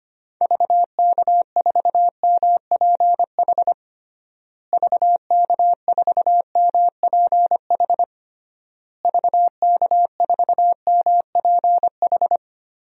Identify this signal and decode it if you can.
Morse code, heard as a keyed tone, about 25 wpm, 700 Hz. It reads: VK4MPH VK4MPH VK4MPH
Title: DXCC-100 Call Signs - 6 Characters - Repeated 3x - 25wpm